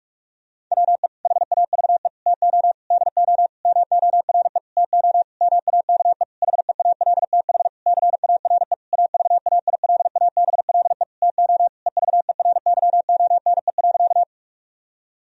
Morse code, heard as a keyed tone, about 46 wpm, 700 Hz. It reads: WE HAVE TO DO MORE TO MAKE HEALTH CARE AVAILABLE TO EVERYONE.
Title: Sentences from Top 1000 Words - Review Part 3 - 45wpm